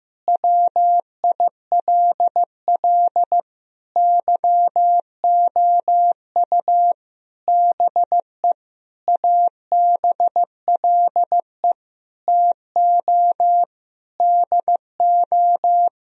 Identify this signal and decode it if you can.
Morse code, heard as a keyed tone, about 15 wpm, 700 Hz. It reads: WILL YOU BE ABLE TO DO
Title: Sentences from Top 1000 Words - Review Part 2 - 15wpm